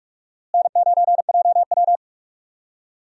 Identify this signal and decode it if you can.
Morse code, heard as a keyed tone, about 43 wpm, 700 Hz. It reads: N9JW
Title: USA Call Signs - 4 Characters - Repeated 3x - 45wpm